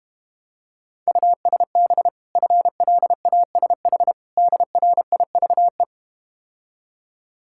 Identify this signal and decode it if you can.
Morse code, heard as a keyed tone, about 32 wpm, 700 Hz. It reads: USB FLASH DRIVE